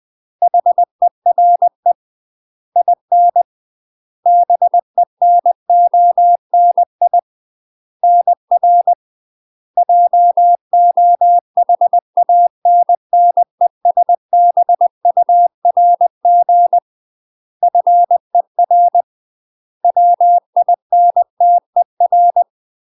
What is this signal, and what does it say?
Morse code, heard as a keyed tone, about 20 wpm, 700 Hz. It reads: HERE IN BENONI NR JOHANNESBURG FER WINTER
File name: UK MASTER QSOs TRAINING / 20WPM / UK MASTER QSOs 0029